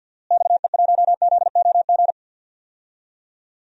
Morse code, heard as a keyed tone, about 49 wpm, 700 Hz. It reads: XE1ZOG
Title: DXCC-100 Call Signs - 6 Characters - Repeated 3x - 50wpm